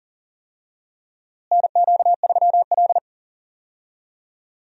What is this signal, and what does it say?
Morse code, heard as a keyed tone, about 37 wpm, 700 Hz. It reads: NQ3L
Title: USA Call Signs - 4 Characters - Repeated 2x - 40wpm